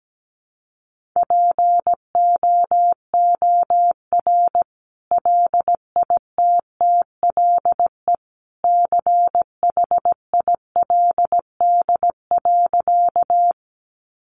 Morse code, heard as a keyed tone, about 17 wpm, 700 Hz. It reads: POOR LITTLE CHILD.
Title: Sentences from Top 1000 Words 17wpm